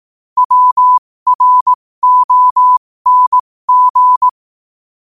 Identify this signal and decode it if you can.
Morse code, heard as a keyed tone, about 18 wpm, spaced as for 15 wpm, 1000 Hz. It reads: WRONG
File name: qotdfiles / 15wpm